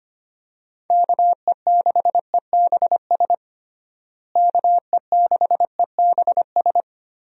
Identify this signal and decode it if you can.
Morse code, heard as a keyed tone, about 25 wpm, 700 Hz. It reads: KE6EBS KE6EBS
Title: USA Call Signs - 6 Characters - Repeated 3x - 25wpm